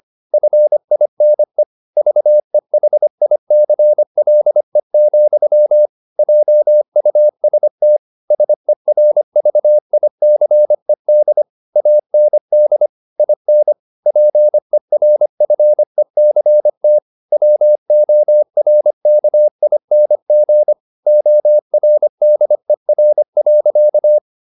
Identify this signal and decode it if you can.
Morse code, heard as a keyed tone, about 25 wpm, 600 Hz. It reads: FINE VEHICLE, JUST SERVICED AND IN PERFECT WORKING ORDER.